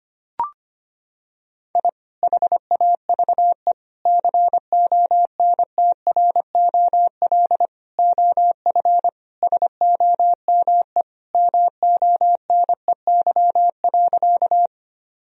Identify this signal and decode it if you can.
Morse code, heard as a keyed tone, about 25 wpm, 700 Hz. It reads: I HAVE CONTROL OF SOME MONEY.
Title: Sentences from Top 1000 Words - Review Part 3 - 25wpm